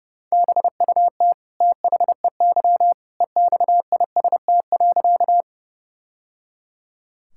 Morse code, heard as a keyed tone, about 30 wpm, 700 Hz. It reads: BUT THEY EXIST.
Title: Sentences from Top 1000 Words - Review Part 5 - 30wpm